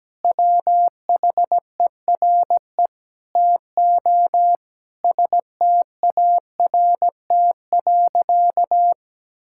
Morse code, heard as a keyed tone, about 17 wpm, 700 Hz. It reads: WHERE TO START.
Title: Sentences from Top 1000 Words - Review Part 5 - 17wpm